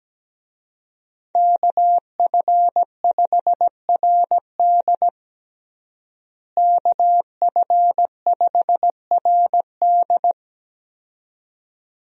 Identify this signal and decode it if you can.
Morse code, heard as a keyed tone, about 17 wpm, 700 Hz. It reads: KF5RD KF5RD
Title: USA Call Signs - 5 Characters - Repeated 2x - 17wpm